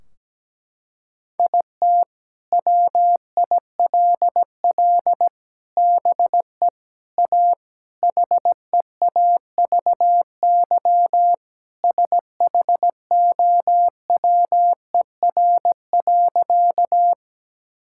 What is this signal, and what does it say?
Morse code, heard as a keyed tone, about 17 wpm, 700 Hz. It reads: IT WILL BE A HEAVY SHOWER.
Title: Sentences from Top 1600 Words 17wpm